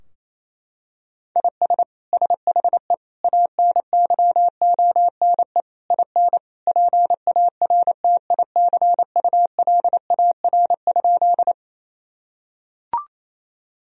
Morse code, heard as a keyed tone, about 27 wpm, 700 Hz. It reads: IS SHE ANYONE IN PARTICULAR?
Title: Sentences from Top 1000 Words - Review Part 4 - 28wpm